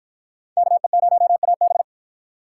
Morse code, heard as a keyed tone, about 47 wpm, 700 Hz. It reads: KE0AB